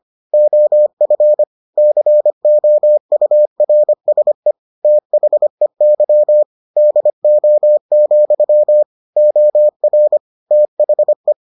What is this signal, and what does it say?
Morse code, heard as a keyed tone, about 25 wpm, 600 Hz. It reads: OF COURSE THEY DO, OR THE